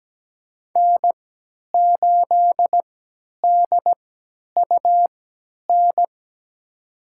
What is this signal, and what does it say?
Morse code, heard as a keyed tone, about 17 wpm, spaced as for 10 wpm, 700 Hz. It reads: N8DUN
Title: USA Call Signs - 5 Characters - 3x Character Spacing - 17wpm